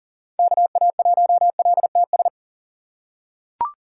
Morse code, heard as a keyed tone, about 40 wpm, 700 Hz. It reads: KA1LTS
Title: USA Call Signs - 6 Characters - 6x Character Spacing - 40wpm